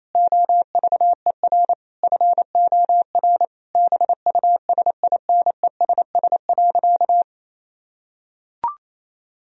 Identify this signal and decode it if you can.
Morse code, heard as a keyed tone, about 28 wpm, 700 Hz. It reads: OVER FOR BUSINESS.
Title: Sentences from Top 1000 Words - Review Part 3 - 28wpm